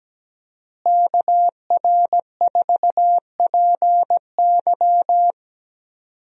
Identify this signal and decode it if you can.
Morse code, heard as a keyed tone, about 17 wpm, 700 Hz. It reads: KR4PY
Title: USA Call Signs - 5 Characters - Repeated 2x - 17wpm